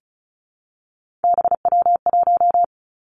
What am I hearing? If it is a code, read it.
Morse code, heard as a keyed tone, about 36 wpm, 700 Hz. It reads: BW1
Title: Three Challenging Letter-Number Combinations 35wpm